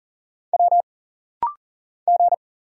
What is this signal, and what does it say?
Morse code, heard as a keyed tone, about 39 wpm, 700 Hz. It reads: W G
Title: Single Letters - Rapid-Fire 40wpm